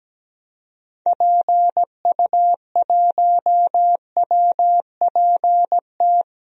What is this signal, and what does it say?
Morse code, heard as a keyed tone, about 17 wpm, 700 Hz. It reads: PU1WPT
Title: DXCC-100 Call Signs - 6 Characters - 17wpm